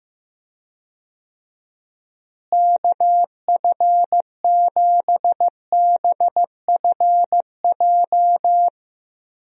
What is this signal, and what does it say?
Morse code, heard as a keyed tone, about 15 wpm, 700 Hz. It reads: KF7BFJ
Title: USA Call Signs - 6 Characters - Repeated 2x - 15wpm